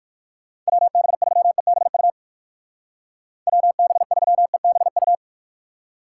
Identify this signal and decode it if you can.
Morse code, heard as a keyed tone, about 59 wpm, 700 Hz. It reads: WB3EBU WB3EBU